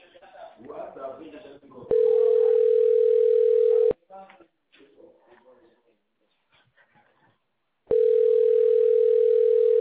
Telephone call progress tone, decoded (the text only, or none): ringback tone